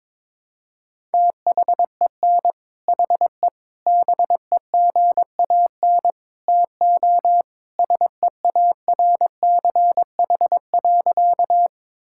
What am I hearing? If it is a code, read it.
Morse code, heard as a keyed tone, about 22 wpm, 700 Hz. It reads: THEN HE BEGAN TO SEARCH.
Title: Sentences from Top 1000 Words - Review Part 5 - 22wpm